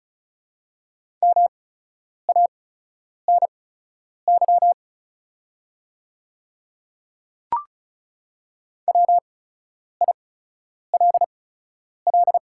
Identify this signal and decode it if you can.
Morse code, heard as a keyed tone, about 36 wpm, spaced as for 10 wpm, 690 Hz. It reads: MANY WILL
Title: 8x Character Spacing - Top 200 Words - 35wpm